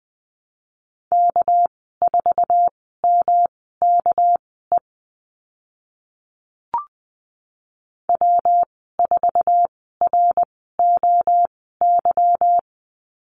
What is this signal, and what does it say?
Morse code, heard as a keyed tone, about 20 wpm, spaced as for 14 wpm, 700 Hz. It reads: K4MKE W4ROY
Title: 200 Lesson 32g - Letter k - Call Signs 20wpm